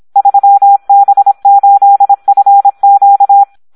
Morse code, heard as a keyed tone, about 26 wpm, 800 Hz. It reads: 3B8FQ
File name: Audio practica CW / SD 1 26